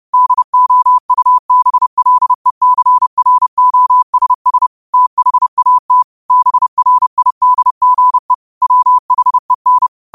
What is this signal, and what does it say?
Morse code, heard as a keyed tone, about 30 wpm, 1000 Hz. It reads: DOUBLECROSS THAT BRIDGE WHEN
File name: 30wpm